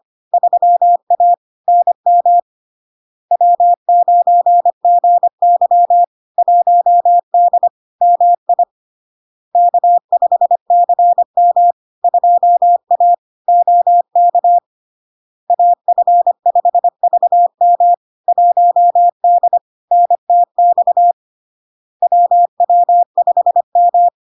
Morse code, heard as a keyed tone, about 25 wpm, 700 Hz. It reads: 3A NM W9GY 1D MI K5CM 2A OK AF5VM 1D NTX WW5M
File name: LARK-FD-2022-Log-in-CW-25wpm-700-Hz